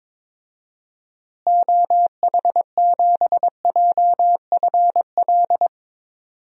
Morse code, heard as a keyed tone, about 22 wpm, 700 Hz. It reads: OH7JFL
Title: DXCC-100 Call Signs - 6 Characters - Repeated 3x - 22wpm